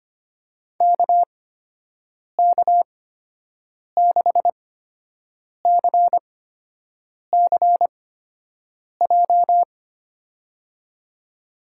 Morse code, heard as a keyed tone, about 25 wpm, spaced as for 7 wpm, 700 Hz. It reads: KK6CCJ
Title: USA Call Signs - 6 Characters - 8x Character Spacing - 25wpm